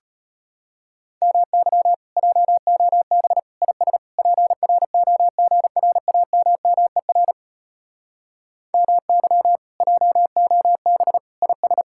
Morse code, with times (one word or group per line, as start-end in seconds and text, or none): 1.2-1.9 MY
2.2-3.4 JOB
3.6-4.0 IS
4.2-7.3 PROGRAMMER
8.7-9.6 MY
9.8-11.2 JOB
11.4-11.8 IS